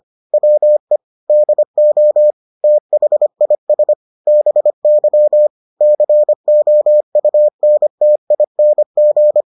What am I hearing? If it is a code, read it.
Morse code, heard as a keyed tone, about 25 wpm, 600 Hz. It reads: WE DO THIS BY COUNTING